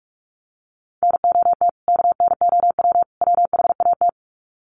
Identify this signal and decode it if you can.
Morse code, heard as a keyed tone, about 44 wpm, 700 Hz. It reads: NOT KNOW WHAT
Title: Sets of 3 Words 45wpm